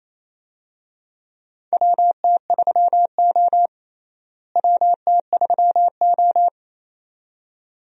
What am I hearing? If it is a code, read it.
Morse code, heard as a keyed tone, about 28 wpm, 700 Hz. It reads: WT3O WT3O
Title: USA Call Signs - 4 Characters - Repeated 2x - 28wpm